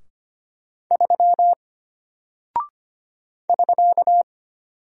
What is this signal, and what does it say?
Morse code, heard as a keyed tone, about 25 wpm, 700 Hz. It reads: 3 <SK>
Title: Single Character (Letter:Number:Punctuation:Prosign) Alt 25wpm